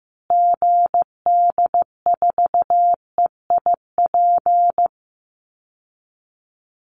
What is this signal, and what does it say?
Morse code, heard as a keyed tone, about 15 wpm, 700 Hz. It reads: GD4EIP
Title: Call Signs Repeated 3x - 15wpm